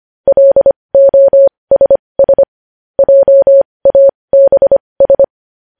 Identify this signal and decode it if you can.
Morse code, heard as a keyed tone, about 25 wpm, spaced as for 20 wpm, 550 Hz. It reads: LOSS JABS
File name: podcast / News headlines-1x-25x20